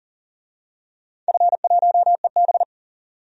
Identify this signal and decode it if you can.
Morse code, heard as a keyed tone, about 40 wpm, 700 Hz. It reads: F1EB
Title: DXCC-100 Call Signs - 4 Characters - 40wpm